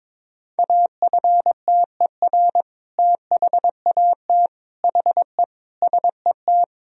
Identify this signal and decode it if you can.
Morse code, heard as a keyed tone, about 22 wpm, 700 Hz. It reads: AFTER THAT HE SET
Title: Sentences from Top 1000 Words - Review Part 1 - 22wpm